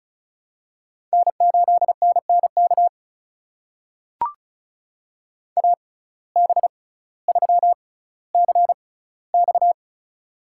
Morse code, heard as a keyed tone, about 36 wpm, spaced as for 18 wpm, 700 Hz. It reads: N8NNK AB3CX
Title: USA Call Signs - 5 Characters - 6x Character Spacing - 35wpm